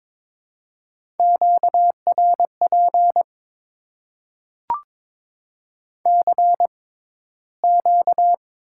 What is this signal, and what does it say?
Morse code, heard as a keyed tone, about 22 wpm, spaced as for 13 wpm, 700 Hz. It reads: QRP CQ
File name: Top 90 QSO Elements - 6x Character Spacing - 22wpm